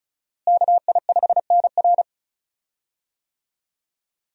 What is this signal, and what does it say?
Morse code, heard as a keyed tone, about 35 wpm, 700 Hz. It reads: KI5NR
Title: USA Call Signs - 5 Characters - Repeated 2x - 35wpm